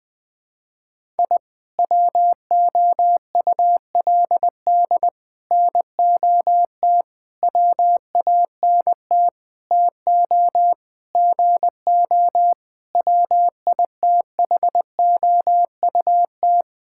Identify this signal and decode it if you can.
Morse code, heard as a keyed tone, about 20 wpm, 700 Hz. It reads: I WOULD NOT WANT TO GO WITHOUT
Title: Sentences with 5-Word Phrases - 20wpm